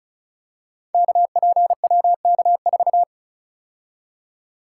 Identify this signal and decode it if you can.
Morse code, heard as a keyed tone, about 36 wpm, 700 Hz. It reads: KPWK4
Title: Code Groups - 5 Letter:Numbers - 35wpm